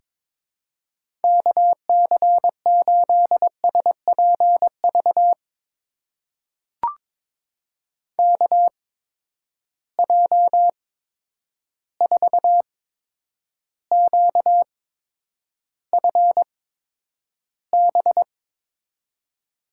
Morse code, heard as a keyed tone, about 22 wpm, spaced as for 9 wpm, 700 Hz. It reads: KC8SPV KJ4QFB
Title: USA Call Signs - 6 Characters - 8x Character Spacing - 22wpm